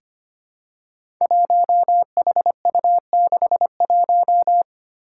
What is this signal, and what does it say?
Morse code, heard as a keyed tone, about 25 wpm, 700 Hz. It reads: 1HU61
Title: Five Challenging Letter-Number Combinations 25wpm